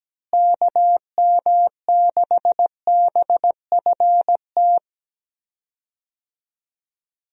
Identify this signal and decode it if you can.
Morse code, heard as a keyed tone, about 17 wpm, 700 Hz. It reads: KM6BFT
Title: USA Call Signs - 6 Characters - Repeated 3x - 17wpm